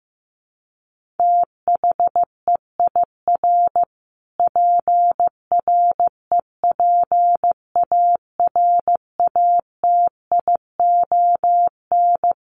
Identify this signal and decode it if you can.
Morse code, heard as a keyed tone, about 15 wpm, 700 Hz. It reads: THEIR PREPARATION
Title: Sentences from Top 1400 Words 15wpm